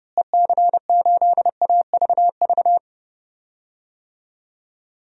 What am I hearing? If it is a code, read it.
Morse code, heard as a keyed tone, about 32 wpm, 700 Hz. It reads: EC8AVV
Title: DXCC-100 Call Signs - 6 Characters - Repeated 3x - 30wpm